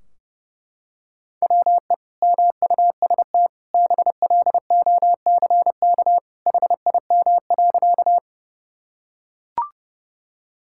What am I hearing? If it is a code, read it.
Morse code, heard as a keyed tone, about 30 wpm, 700 Hz. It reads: WE MUST BLOCK HIM.
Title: Sentences from Top 1600 Words 30wpm